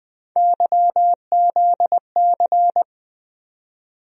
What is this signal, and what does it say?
Morse code, heard as a keyed tone, about 20 wpm, 700 Hz. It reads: YZC
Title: USA Call Sign Postfixes - 3 Letters 20wpm